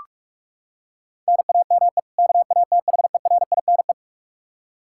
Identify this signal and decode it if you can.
Morse code, heard as a keyed tone, about 42 wpm, 700 Hz. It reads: NAME KATHERINE
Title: Practice QSO - Exchange 1 - Beginning - 45wpm